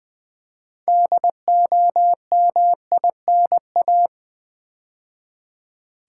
Morse code, heard as a keyed tone, about 20 wpm, 700 Hz. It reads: DOMINA
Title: Build A Word - 2 Letters At A Time - 8-Letter Words - 20wpm